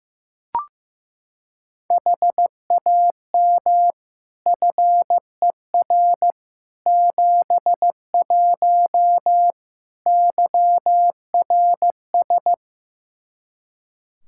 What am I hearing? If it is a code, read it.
Morse code, heard as a keyed tone, about 16 wpm, 700 Hz. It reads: HAM FER 71 YRS
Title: K1USN SST QSO Practice 15wpm